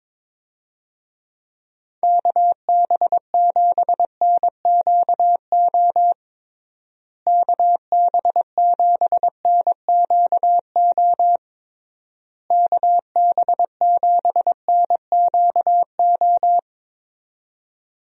Morse code, heard as a keyed tone, about 22 wpm, 700 Hz. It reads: KB7NQO KB7NQO KB7NQO
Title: USA Call Signs - 6 Characters - Repeated 3x - 22wpm